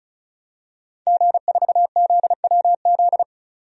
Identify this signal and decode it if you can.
Morse code, heard as a keyed tone, about 36 wpm, 700 Hz. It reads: G4ZWZ
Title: DXCC-100 Call Signs - 5 Characters - 35wpm